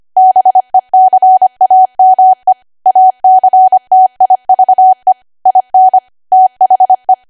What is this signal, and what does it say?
Morse code, heard as a keyed tone, about 25 wpm, 750 Hz. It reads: BECAME ACTIVE IN THE
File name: Archive / 25 WPM / 25WPM